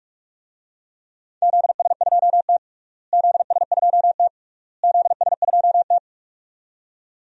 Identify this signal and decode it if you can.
Morse code, heard as a keyed tone, about 45 wpm, 700 Hz. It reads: ZS2T ZS2T ZS2T
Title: DXCC-100 Call Signs - 4 Characters - Repeated 3x - 45wpm